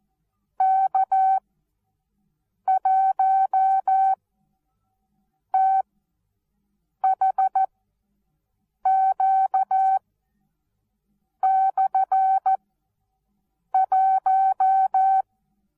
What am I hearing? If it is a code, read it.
Morse code, heard as a keyed tone, about 14 wpm, spaced as for 6 wpm, 770 Hz. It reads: K1THQ/1
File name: Onlinelehrgang / morsen / DJ4UF Morsen19